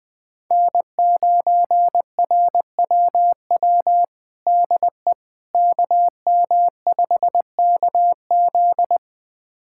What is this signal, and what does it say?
Morse code, heard as a keyed tone, about 20 wpm, 700 Hz. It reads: N9RWW DE KM5KZ
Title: USA Callsign DE Callsign - 5 Characters - 20wpm